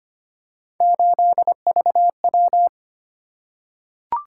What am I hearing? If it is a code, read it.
Morse code, heard as a keyed tone, about 25 wpm, 700 Hz. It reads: 8VW
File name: Three Letter-Number Combinations - 25wpm